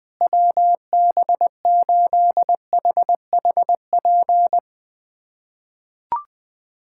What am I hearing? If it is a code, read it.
Morse code, heard as a keyed tone, about 21 wpm, 700 Hz. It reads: WB8HHP
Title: USA Callsign DE Callsign - 6 Characters - 20wpm